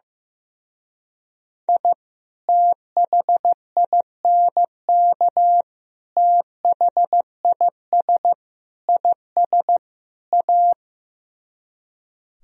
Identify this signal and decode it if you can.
Morse code, heard as a keyed tone, about 15 wpm, 700 Hz. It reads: I THINK THIS IS A
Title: Speed-Racing - Sets of 5 Words 15wpm